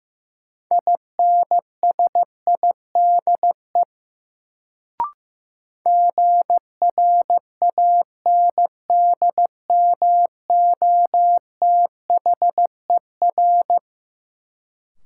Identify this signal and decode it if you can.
Morse code, heard as a keyed tone, about 15 wpm, 700 Hz. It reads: INSIDE GRANDMOTHER
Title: Sentences from Top 1200 Words - New Word Drill Alt - 15wpm